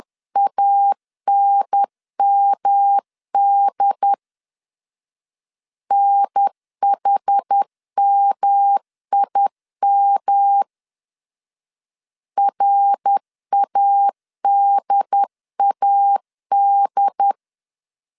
Morse code, heard as a keyed tone, about 10 wpm, 800 Hz. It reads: ANMD NHMIM RADAD
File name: CW / DET / 1 3 1500C S B 10wpm 3335